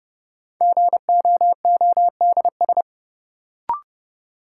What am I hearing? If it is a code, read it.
Morse code, heard as a keyed tone, about 32 wpm, 700 Hz. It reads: GOODS
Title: Sentences from Top 1000 Words - Review Part 6 - New Word Drill - 30wpm